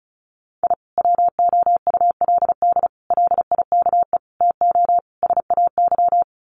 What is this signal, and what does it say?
Morse code, heard as a keyed tone, about 34 wpm, 700 Hz. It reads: I WOULD LIKE TO SAY
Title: Sets of 5 Words Encore2 35wpm